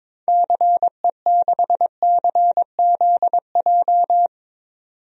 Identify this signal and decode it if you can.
Morse code, heard as a keyed tone, about 22 wpm, 700 Hz. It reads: CE6CZJ
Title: DXCC-100 Call Signs - 6 Characters - 22wpm